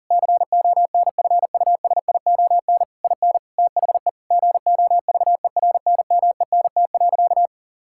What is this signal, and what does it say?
Morse code, heard as a keyed tone, about 40 wpm, 700 Hz. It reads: CONFUSION IN THE GOVERNMENT.
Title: Sentences from Top 1600 Words 40wpm